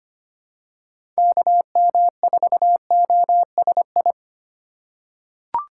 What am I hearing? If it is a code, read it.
Morse code, heard as a keyed tone, about 25 wpm, 700 Hz. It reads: KM4OSI
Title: USA Call Signs - 6x Character Spacing - 25wpm